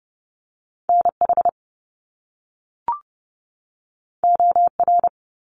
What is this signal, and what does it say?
Morse code, heard as a keyed tone, about 30 wpm, 700 Hz. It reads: NH OR
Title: US States 30wpm